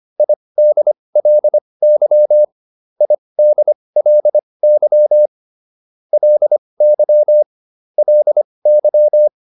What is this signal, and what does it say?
Morse code, heard as a keyed tone, about 25 wpm, spaced as for 20 wpm, 600 Hz. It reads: IDLY IDLY LY LY